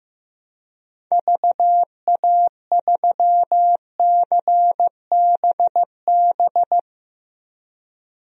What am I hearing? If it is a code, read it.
Morse code, heard as a keyed tone, about 15 wpm, 700 Hz. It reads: VA3CBB
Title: DXCC-100 Call Signs - 6 Characters - 15wpm